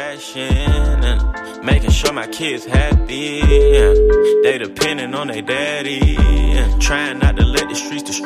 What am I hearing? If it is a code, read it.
Ringback tone (Europe, 425 Hz), with noise.